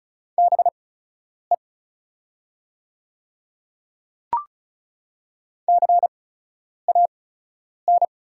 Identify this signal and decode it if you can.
Morse code, heard as a keyed tone, about 34 wpm, spaced as for 10 wpm, 690 Hz. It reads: BE CAN
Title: 8x Character Spacing - Top 100 Words - 35wpm